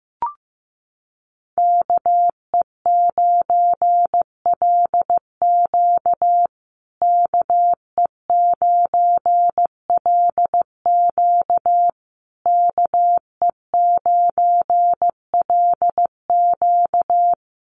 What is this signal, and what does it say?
Morse code, heard as a keyed tone, about 15 wpm, 700 Hz. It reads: KE9LQ KE9LQ KE9LQ
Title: USA Call Signs Repeated 3x 15wpm